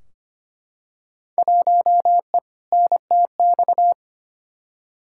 Morse code, heard as a keyed tone, about 25 wpm, 700 Hz. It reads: 1E NTX
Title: Field Day Exchanges - 25wpm